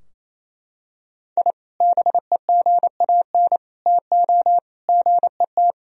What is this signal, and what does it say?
Morse code, heard as a keyed tone, about 28 wpm, 700 Hz. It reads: I BEGAN TO GET
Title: Sentences from Top 1000 Words - Review Part 4 - 28wpm